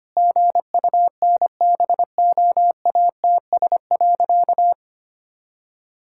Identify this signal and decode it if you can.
Morse code, heard as a keyed tone, about 25 wpm, 700 Hz. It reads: GUNBOATS.
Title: Pangram Challenge v2 - 25wpm